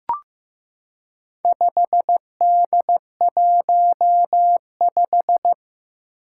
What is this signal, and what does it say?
Morse code, heard as a keyed tone, about 15 wpm, 700 Hz. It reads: 5D15
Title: Four Challenging Letter-Number Combinations 15wpm